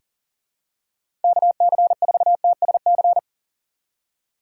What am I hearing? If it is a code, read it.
Morse code, heard as a keyed tone, about 40 wpm, 700 Hz. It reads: KC4TSC
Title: USA Call Signs - 6 Characters - 6x Character Spacing - 40wpm